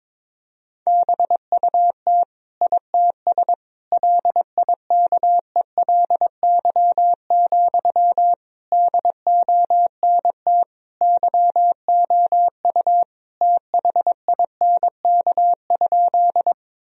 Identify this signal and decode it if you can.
Morse code, heard as a keyed tone, about 22 wpm, 700 Hz. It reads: BUT ITS LIKELY, DONT YOU THINK?
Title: Sentences from Top 1000 Words - Review Part 4 - 22wpm